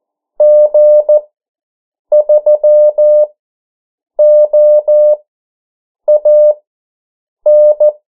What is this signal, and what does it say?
Morse code, heard as a keyed tone, about 14 wpm, spaced as for 7 wpm, 600 Hz. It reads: G3OAN
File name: morse mp3 / 09 urok 8